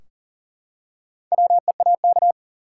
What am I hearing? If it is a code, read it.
Morse code, heard as a keyed tone, about 40 wpm, 700 Hz. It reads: WEAK
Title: Four Letter Words - 40wpm